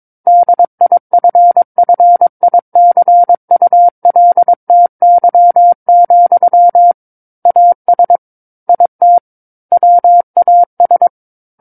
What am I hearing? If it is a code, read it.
Morse code, heard as a keyed tone, about 22 wpm, 700 Hz. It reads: DIFFICULTY, AS IT WAS